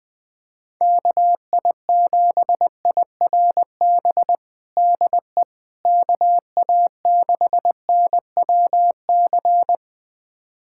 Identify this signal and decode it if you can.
Morse code, heard as a keyed tone, about 21 wpm, 700 Hz. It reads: KI7IRB DE KA6NWC
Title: USA Callsign DE Callsign - 6 Characters - 20wpm